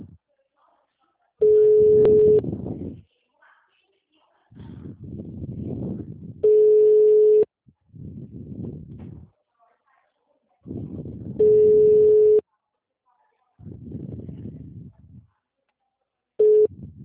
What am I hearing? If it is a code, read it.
Ringback tone (Europe, 425 Hz), with speech.